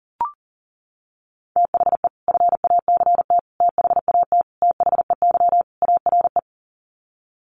Morse code, heard as a keyed tone, about 37 wpm, 700 Hz. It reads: THE FACT THAT THEY ARE
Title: Sets of 5 Words Encore2 40wpm